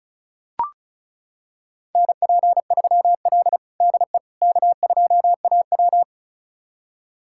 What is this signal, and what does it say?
Morse code, heard as a keyed tone, about 36 wpm, 700 Hz. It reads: NP3L DE K2AW
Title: USA Callsign DE Callsign - 4 Characters - 35wpm